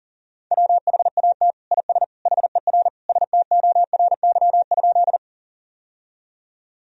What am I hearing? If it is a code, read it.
Morse code, heard as a keyed tone, about 39 wpm, 700 Hz. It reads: WHAT IS HER STORY?
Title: Sentences from Top 1000 Words - Review Part 1 - 40wpm